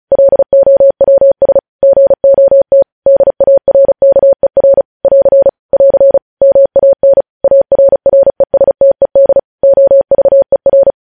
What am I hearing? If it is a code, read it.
Morse code, heard as a keyed tone, about 36 wpm, 550 Hz. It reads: LOWS GOT DARKER <AR> <AR> MAN ARRESTED OVER
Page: news headlines in morse code at 35 wpm